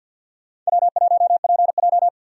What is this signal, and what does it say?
Morse code, heard as a keyed tone, about 46 wpm, 700 Hz. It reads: W1PJ
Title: USA Call Signs - 4 Characters - 50wpm